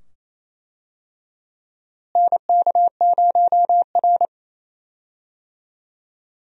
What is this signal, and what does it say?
Morse code, heard as a keyed tone, about 28 wpm, 700 Hz. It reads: NK0R
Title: USA Call Signs - 4 Characters - Repeated 2x - 28wpm